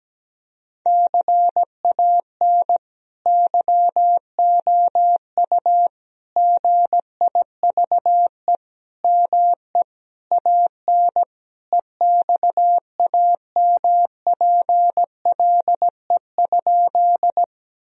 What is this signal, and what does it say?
Morse code, heard as a keyed tone, about 17 wpm, 700 Hz. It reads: CAN YOU GIVE ME AN EXAMPLE?
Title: Sentences from Top 1000 Words - Review Part 1 - 17wpm